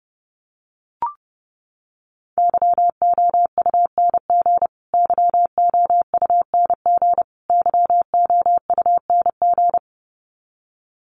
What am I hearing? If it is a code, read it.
Morse code, heard as a keyed tone, about 30 wpm, 700 Hz. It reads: YOUNG YOUNG YOUNG
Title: Top 500 Words Repeated 30wpm